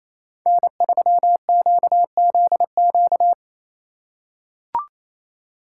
Morse code, heard as a keyed tone, about 28 wpm, 700 Hz. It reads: N3QZQ
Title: USA Call Signs - 4x Character Spacing - 28wpm